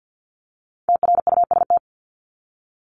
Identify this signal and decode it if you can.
Morse code, heard as a keyed tone, about 50 wpm, 700 Hz. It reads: TRUST
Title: Top 1751-2000 Words 50wpm